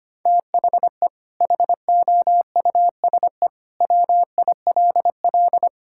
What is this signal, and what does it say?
Morse code, heard as a keyed tone, about 25 wpm, 700 Hz. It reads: THE HOUSE WILL
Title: Sentences from Top 1600 Words 25wpm